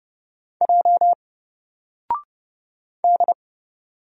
Morse code, heard as a keyed tone, about 30 wpm, 700 Hz. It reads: J D
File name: Single-Letter-Number-30wpm